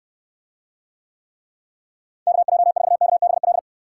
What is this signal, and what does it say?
Morse code, heard as a keyed tone, about 69 wpm, 700 Hz. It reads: KJ4KBL